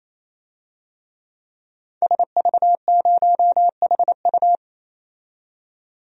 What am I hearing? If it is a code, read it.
Morse code, heard as a keyed tone, about 28 wpm, 700 Hz. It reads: SV0HU
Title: DXCC-100 Call Signs - 5 Characters - 28wpm